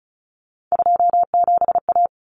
Morse code, heard as a keyed tone, about 34 wpm, 700 Hz. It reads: 27A
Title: Common Cut Numbers - 3-Digit 35wpm